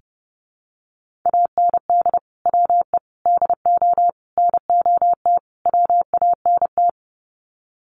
Morse code, heard as a keyed tone, about 30 wpm, 700 Hz. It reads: AND WE DO NOT WANT
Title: Sets of 5 Words Encore2 30wpm